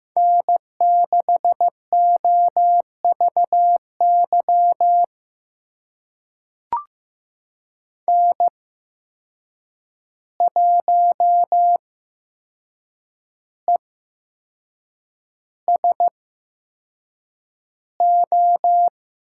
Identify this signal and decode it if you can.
Morse code, heard as a keyed tone, about 15 wpm, spaced as for 6 wpm, 700 Hz. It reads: N6OVY N1ESO
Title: USA Call Signs - 5 Characters - 8x Character Spacing - 15wpm